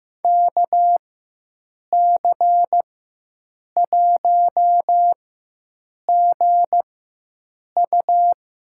Morse code, heard as a keyed tone, about 15 wpm, spaced as for 7 wpm, 700 Hz. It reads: KC1GU
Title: USA Call Signs - 6 Characters - 4x Character Spacing - 15wpm